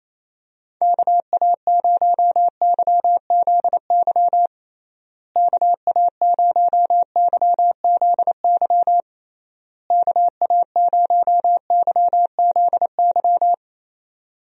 Morse code, heard as a keyed tone, about 28 wpm, 700 Hz. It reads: KA0YZY KA0YZY KA0YZY
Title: USA Call Signs - 6 Characters - Repeated 3x - 28wpm